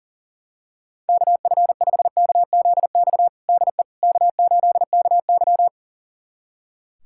Morse code, heard as a keyed tone, about 40 wpm, 700 Hz. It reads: KF5KZX DE K8KY
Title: USA Callsign DE Callsign 40wpm